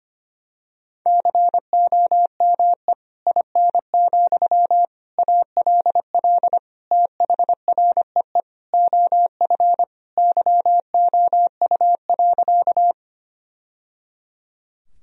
Morse code, heard as a keyed tone, about 25 wpm, 700 Hz. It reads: COME IN, ALL THREE OF YOU.
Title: Sentences from Top 1000 Words - Review Part 1 - 25wpm